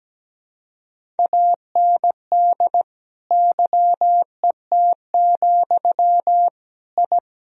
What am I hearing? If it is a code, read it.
Morse code, heard as a keyed tone, about 17 wpm, 700 Hz. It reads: AND YET, I
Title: Sentences from Top 1000 Words - Review Part 1 - 17wpm